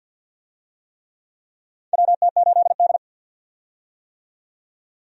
Morse code, heard as a keyed tone, about 50 wpm, 700 Hz. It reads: WT8D
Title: USA Call Signs - 4 Characters - 50wpm